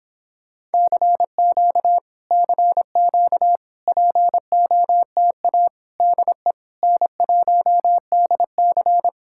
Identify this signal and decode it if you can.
Morse code, heard as a keyed tone, about 26 wpm, 700 Hz. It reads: CQ CQ POTA DE N1DC